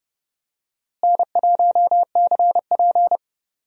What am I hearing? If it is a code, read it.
Morse code, heard as a keyed tone, about 29 wpm, 700 Hz. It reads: N1CP
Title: USA Call Signs - 4 Characters - Repeated 2x - 30wpm